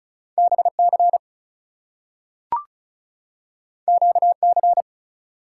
Morse code, heard as a keyed tone, about 36 wpm, 700 Hz. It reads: BC QC
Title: US States and Canadian Provinces 35wpm